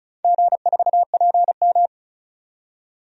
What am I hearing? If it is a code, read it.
Morse code, heard as a keyed tone, about 34 wpm, 700 Hz. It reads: G4PM
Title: DXCC-100 Call Signs - 4 Characters - 35wpm